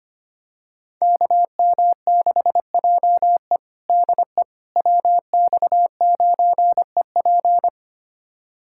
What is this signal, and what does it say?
Morse code, heard as a keyed tone, about 25 wpm, 700 Hz. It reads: KM6JE DE WX9EP
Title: USA Callsign DE Callsign 25wpm